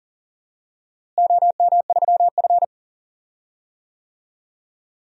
Morse code, heard as a keyed tone, about 40 wpm, 700 Hz. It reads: OM3F
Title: DXCC-100 Call Signs - 4 Characters - Repeated 2x - 40wpm